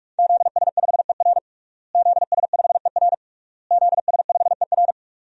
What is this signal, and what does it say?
Morse code, heard as a keyed tone, about 41 wpm, 700 Hz. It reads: ZS5ER ZS5ER ZS5ER
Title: DXCC-100 Call Signs - 5 Characters - Repeated 3x - 45wpm